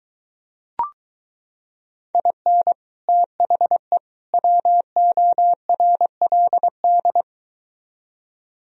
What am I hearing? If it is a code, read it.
Morse code, heard as a keyed tone, about 22 wpm, 700 Hz. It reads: IN THE WORLD